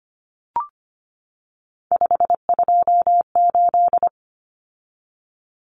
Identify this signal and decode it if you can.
Morse code, heard as a keyed tone, about 25 wpm, 700 Hz. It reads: 528
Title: Common Cut Numbers - 3-Digit 25wpm